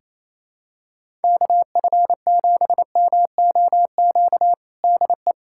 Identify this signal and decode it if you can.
Morse code, heard as a keyed tone, about 28 wpm, 700 Hz. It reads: KF7MOQ DE
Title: USA Callsign DE Callsign - 6 Characters - 28wpm